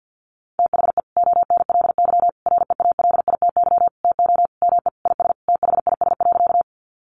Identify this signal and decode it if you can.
Morse code, heard as a keyed tone, about 50 wpm, 700 Hz. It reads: THE ONLY REALITY TO ME IS THIS.
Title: Sentences from Top 1200 Words 50wpm